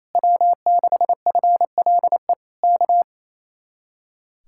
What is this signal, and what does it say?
Morse code, heard as a keyed tone, about 26 wpm, 700 Hz. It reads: W6FLE K
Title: Speed-Racing Common QSO Phrases - 28wpm